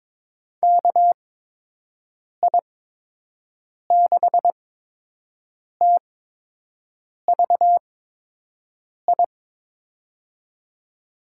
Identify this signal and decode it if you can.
Morse code, heard as a keyed tone, about 21 wpm, spaced as for 6 wpm, 700 Hz. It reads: KI6TVI
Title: USA Call Signs - 6 Characters - 8x Character Spacing - 22wpm